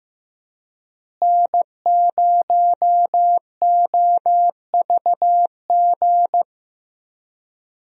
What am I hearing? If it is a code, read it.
Morse code, heard as a keyed tone, about 16 wpm, 700 Hz. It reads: N0OVG
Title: USA Call Signs - 5 Characters - Repeated 3x - 15wpm